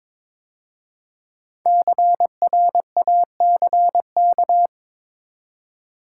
Morse code, heard as a keyed tone, about 22 wpm, 700 Hz. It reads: CRACK
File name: 5 Letter Words - First 3 Letters Given - 22wpm